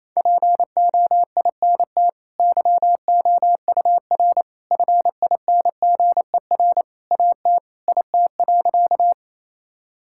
Morse code, heard as a keyed tone, about 28 wpm, 700 Hz. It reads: POINT YOUR FINGER AT IT.
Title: Sentences from Top 1600 Words 28wpm